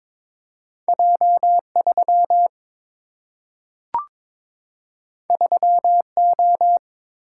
Morse code, heard as a keyed tone, about 22 wpm, 700 Hz. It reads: J 3 3 O
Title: Single-Letter Single-Number 22wpm